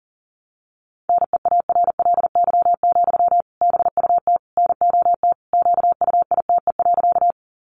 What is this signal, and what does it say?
Morse code, heard as a keyed tone, about 40 wpm, 700 Hz. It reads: NEARLY, BUT NOT QUITE.
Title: Sentences from Top 1300 Words 40wpm